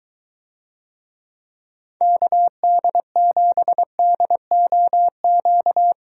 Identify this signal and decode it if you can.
Morse code, heard as a keyed tone, about 23 wpm, 700 Hz. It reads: KD7DOQ